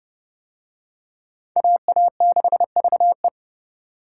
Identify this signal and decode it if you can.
Morse code, heard as a keyed tone, about 29 wpm, 700 Hz. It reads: AA6VE